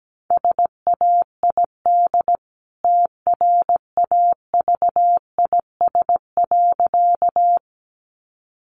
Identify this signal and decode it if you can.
Morse code, heard as a keyed tone, about 17 wpm, 700 Hz. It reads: SAID TRAVIS.
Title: A Sound Of Thunder 17wpm